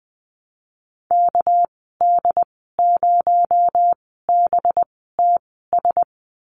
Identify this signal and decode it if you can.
Morse code, heard as a keyed tone, about 20 wpm, spaced as for 15 wpm, 700 Hz. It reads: KD0BTS
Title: 219 Lesson 35g - Number 0 - Call Signs 20wpm